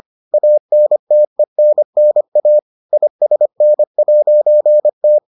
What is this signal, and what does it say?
Morse code, heard as a keyed tone, about 25 wpm, 600 Hz. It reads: ANTENNA ISN'T